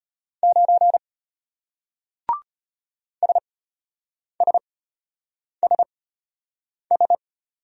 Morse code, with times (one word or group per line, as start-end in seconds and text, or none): 0.4-1.0 9
3.2-7.2 SSSS